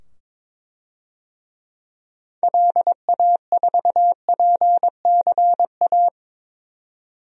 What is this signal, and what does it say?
Morse code, heard as a keyed tone, about 22 wpm, 700 Hz. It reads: LA4PCA
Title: DXCC-100 Call Signs - 6 Characters - 22wpm